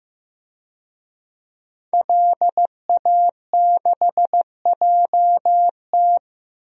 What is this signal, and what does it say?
Morse code, heard as a keyed tone, about 16 wpm, 700 Hz. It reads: LA6JT
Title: DXCC-100 Call Signs - 5 Characters - Repeated 3x - 15wpm